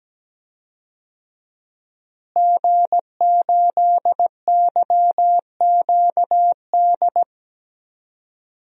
Morse code, heard as a keyed tone, about 17 wpm, 700 Hz. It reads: G8YQD
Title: DXCC-100 Call Signs - 5 Characters - Repeated 2x - 17wpm